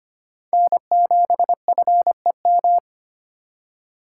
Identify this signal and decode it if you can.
Morse code, heard as a keyed tone, about 25 wpm, 700 Hz. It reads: N7FEM
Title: USA Call Signs - 5 Characters - 2x Character Spacing - 25wpm